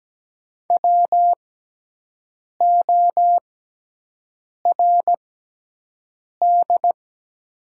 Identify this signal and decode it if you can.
Morse code, heard as a keyed tone, about 17 wpm, spaced as for 6 wpm, 700 Hz. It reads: WORD
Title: 6x Character Spacing - Top 100 Words - 17wpm